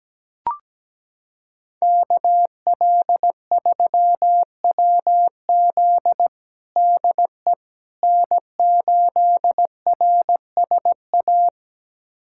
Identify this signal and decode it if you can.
Morse code, heard as a keyed tone, about 17 wpm, 700 Hz. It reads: KL3WZ DE N8RSA
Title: USA Callsign DE Callsign - 5 Characters - 17wpm